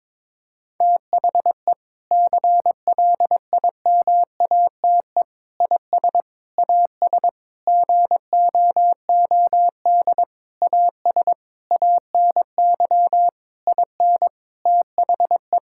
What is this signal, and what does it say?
Morse code, heard as a keyed tone, about 22 wpm, 700 Hz. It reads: THE CLIMATE IS AS GOOD AS ANY IN THE
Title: Sentences from Top 1600 Words 22wpm